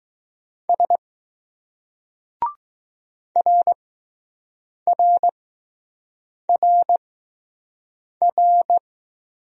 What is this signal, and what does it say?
Morse code, heard as a keyed tone, about 20 wpm, spaced as for 6 wpm, 700 Hz. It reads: S RRRR